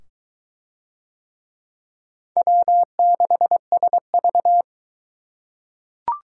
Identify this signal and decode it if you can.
Morse code, heard as a keyed tone, about 21 wpm, 700 Hz. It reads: W6SV